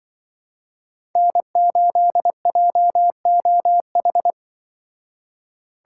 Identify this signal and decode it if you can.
Morse code, heard as a keyed tone, about 25 wpm, 700 Hz. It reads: N8JOH
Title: Speed Racing - USA 1x3 Call Signs 20wpm